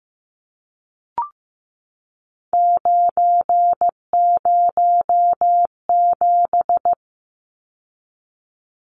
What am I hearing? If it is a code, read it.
Morse code, heard as a keyed tone, about 15 wpm, 700 Hz. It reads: 907
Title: Numbers - 3-Digit 15wpm